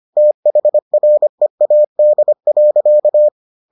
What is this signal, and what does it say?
Morse code, heard as a keyed tone, about 25 wpm, 600 Hz. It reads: THREAD.